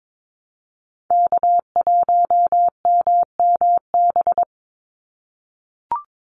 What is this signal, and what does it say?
Morse code, heard as a keyed tone, about 22 wpm, 700 Hz. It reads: K1MMB
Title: Call Signs - 22wpm